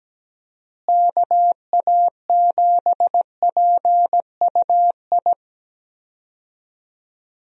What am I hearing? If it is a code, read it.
Morse code, heard as a keyed tone, about 17 wpm, 700 Hz. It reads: KA7PUI
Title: USA Call Signs - 6 Characters - Repeated 3x - 17wpm